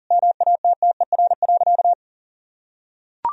Morse code, heard as a keyed tone, about 40 wpm, 700 Hz. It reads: MATTER.
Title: Sentences from Top 1000 Words - Review Part 1 - 40wpm